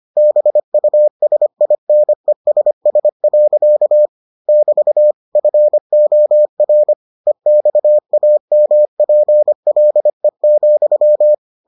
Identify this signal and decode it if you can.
Morse code, heard as a keyed tone, about 25 wpm, 600 Hz. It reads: BUSINESS. <BT> FOR EXAMPLE,